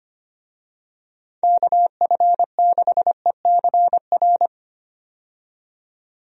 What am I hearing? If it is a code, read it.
Morse code, heard as a keyed tone, about 25 wpm, 700 Hz. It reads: KF6ECR
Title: USA Call Signs - 6 Characters - 25wpm